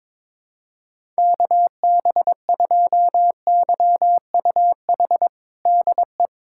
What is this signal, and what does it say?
Morse code, heard as a keyed tone, about 22 wpm, 700 Hz. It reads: KB2YUH DE
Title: USA Callsign DE Callsign - 6 Characters - 22wpm